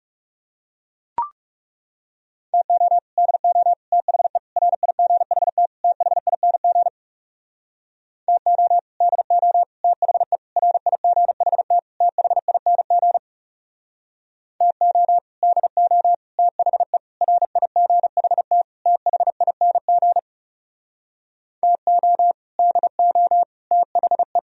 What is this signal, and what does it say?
Morse code, heard as a keyed tone, about 35 wpm, 700 Hz. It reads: TO DO THE RIGHT THING TO DO THE RIGHT THING TO DO THE RIGHT THING TO DO THE
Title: Speed-Racing - Sets of 5 Words 30wpm